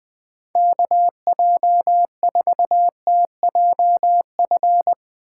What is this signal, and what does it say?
Morse code, heard as a keyed tone, about 20 wpm, 700 Hz. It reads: KJ4TJF
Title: Speed Racing - USA Call Signs 17wpm